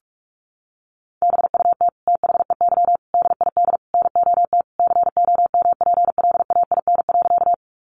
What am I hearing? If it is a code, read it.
Morse code, heard as a keyed tone, about 44 wpm, 700 Hz. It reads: BUT THEY DID NOT COMPLAIN.
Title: Sentences from Top 1300 Words 45wpm